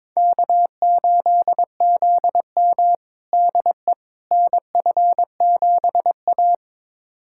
Morse code, heard as a keyed tone, about 22 wpm, 700 Hz. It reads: K8ZM DE NF7A
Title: USA Callsign DE Callsign - 4 Characters - 22wpm